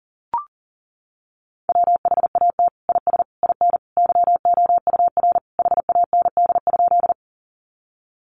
Morse code, heard as a keyed tone, about 40 wpm, 700 Hz. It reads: WHAT IS IN YOUR HAND?
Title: Sentences from Top 200 Words 40wpm